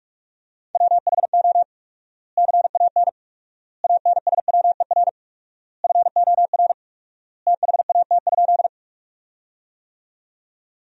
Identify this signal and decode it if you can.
Morse code, heard as a keyed tone, about 46 wpm, 700 Hz. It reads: WHO CAN ANSWER FOR THAT?
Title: 4x Word-Spacing - Sentences from Top 300 Words - 45wpm